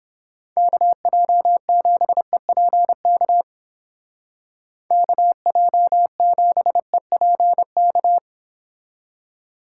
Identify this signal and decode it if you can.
Morse code, heard as a keyed tone, about 29 wpm, 700 Hz. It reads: KJ7EPK KJ7EPK